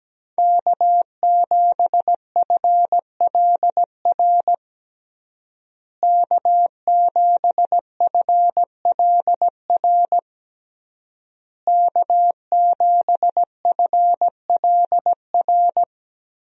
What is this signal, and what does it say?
Morse code, heard as a keyed tone, about 17 wpm, 700 Hz. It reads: K7FLR K7FLR K7FLR
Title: USA Call Signs - 5 Characters - Repeated 3x - 17wpm